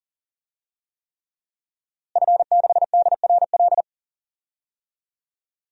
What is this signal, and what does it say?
Morse code, heard as a keyed tone, about 40 wpm, 700 Hz. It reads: F6DRL
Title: DXCC-100 Call Signs - 5 Characters - Repeated 3x - 40wpm